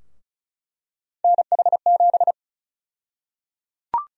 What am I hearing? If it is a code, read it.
Morse code, heard as a keyed tone, about 34 wpm, 700 Hz. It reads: NH7
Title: USA Call Sign Prefixes - Repeated 2x - 35wpm